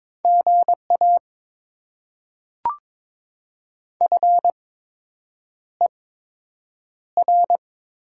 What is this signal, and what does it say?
Morse code, heard as a keyed tone, about 22 wpm, spaced as for 8 wpm, 700 Hz. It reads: GA FER
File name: Top 20 QSO Elements - 8x Character Spacing - 22wpm